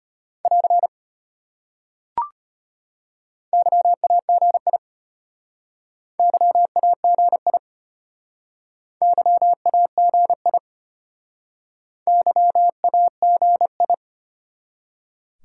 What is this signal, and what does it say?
Morse code, heard as a keyed tone, about 33 wpm, 700 Hz. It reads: <AR> YAGI YAGI YAGI YAGI